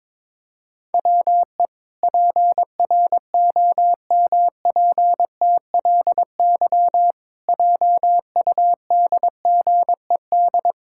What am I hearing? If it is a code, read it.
Morse code, heard as a keyed tone, about 22 wpm, 700 Hz. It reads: WE PROMPTLY JUDGED
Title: Pangram Challenge v2 - 22wpm